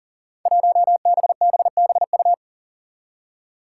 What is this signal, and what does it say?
Morse code, heard as a keyed tone, about 40 wpm, 700 Hz. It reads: 1BBBU
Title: Five Challenging Letter-Number Combinations 40wpm